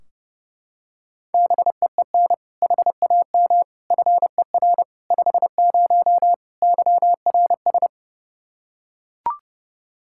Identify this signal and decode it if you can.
Morse code, heard as a keyed tone, about 32 wpm, 700 Hz. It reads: BEEN HAM FER 50 YRS
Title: Practice QSO - Exchange 3 - More Info - 30wpm